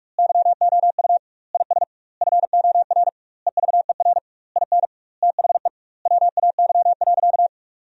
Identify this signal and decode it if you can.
Morse code, heard as a keyed tone, about 43 wpm, 700 Hz. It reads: YOU IS FOR EVER IN THE WAY.
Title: 2x Word-Spacing - Sentences from Top 400 Words - 45wpm